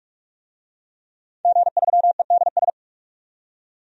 Morse code, heard as a keyed tone, about 46 wpm, 700 Hz. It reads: G3EDS
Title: DXCC-100 Call Signs - 5 Characters - 45wpm